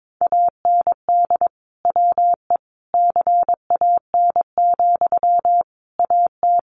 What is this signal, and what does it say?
Morse code, heard as a keyed tone, about 22 wpm, 700 Hz. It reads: AND WE CAN, AT
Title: Sentences from Top 900 Words 22wpm